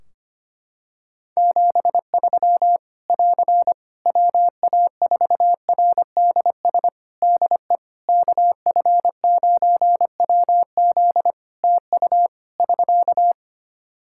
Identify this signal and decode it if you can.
Morse code, heard as a keyed tone, about 25 wpm, 700 Hz. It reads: 73 <AR> WA4RDS DE KF9WZ TU <SK>
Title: K1USN SST QSO Practice 25wpm